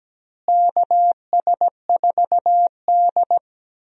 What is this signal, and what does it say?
Morse code, heard as a keyed tone, about 17 wpm, 700 Hz. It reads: KS4D
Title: USA Call Signs - 4 Characters - Repeated 3x - 17wpm